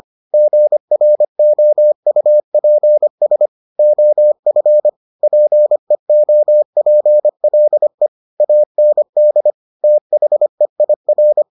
Morse code, heard as a keyed tone, about 25 wpm, 600 Hz. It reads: GROUPS OF PEOPLE AND THEIR